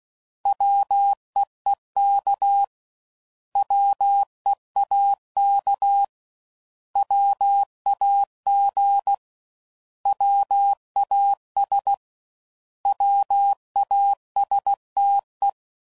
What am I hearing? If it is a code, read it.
Morse code, heard as a keyed tone, about 16 wpm, 800 Hz. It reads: WEEK WEAK WAG WAS WASTE
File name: Prefixes / By First Letter / Ws